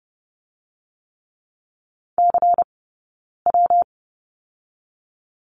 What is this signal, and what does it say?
Morse code, heard as a keyed tone, about 30 wpm, 700 Hz. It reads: C W
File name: Lesson 20d - Number 3 - Two Characters 30wpm